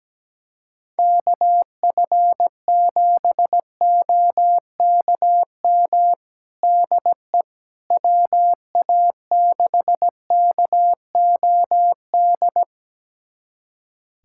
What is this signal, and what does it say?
Morse code, heard as a keyed tone, about 17 wpm, 700 Hz. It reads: KF7OKM DE WA6KOD
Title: USA Callsign DE Callsign - 6 Characters - 17wpm